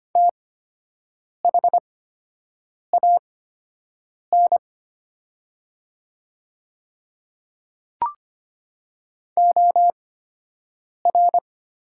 Morse code, heard as a keyed tone, about 25 wpm, spaced as for 7 wpm, 700 Hz. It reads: THAN OR
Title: 8x Character Spacing - Top 100 Words - 25wpm